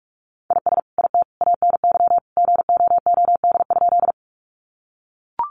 Morse code, heard as a keyed tone, about 44 wpm, 700 Hz. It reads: IS IT ANY GOOD?
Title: Sentences from Top 200 Words 45wpm